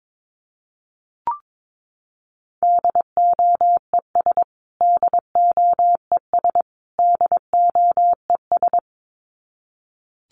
Morse code, heard as a keyed tone, about 22 wpm, 700 Hz. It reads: DOES DOES DOES
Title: Top 500 Words Repeated 22wpm